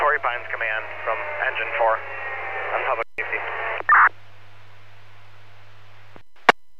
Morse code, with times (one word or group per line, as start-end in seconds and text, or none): none